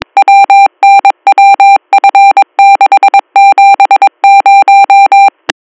Morse code, heard as a keyed tone, about 22 wpm, 790 Hz. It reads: WNWF670